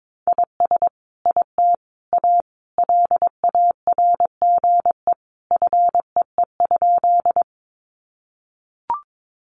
Morse code, heard as a keyed tone, about 22 wpm, 700 Hz. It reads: IS IT A LARGE FEE?
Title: Sentences from Top 1500 Words 22wpm